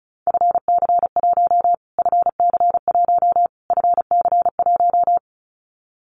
Morse code, heard as a keyed tone, about 34 wpm, 700 Hz. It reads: FC1 FC1 FC1
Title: Callsign Prefixes for Top 25 DXCC - Repeated 3x - 35wpm